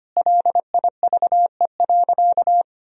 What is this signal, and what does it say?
Morse code, heard as a keyed tone, about 25 wpm, 700 Hz. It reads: LIVE.
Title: Sentences from Top 1000 Words - Review Part 3 - 25wpm